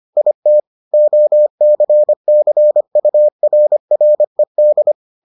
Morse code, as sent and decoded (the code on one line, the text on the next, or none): .. - / --- -.-. -.-. ..- .-. .-. -..
IT OCCURRED